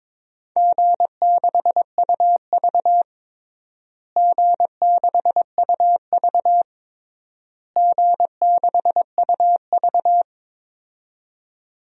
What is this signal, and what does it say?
Morse code, heard as a keyed tone, about 22 wpm, 700 Hz. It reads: G6UV G6UV G6UV
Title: DXCC-100 Call Signs - 4 Characters - Repeated 3x - 22wpm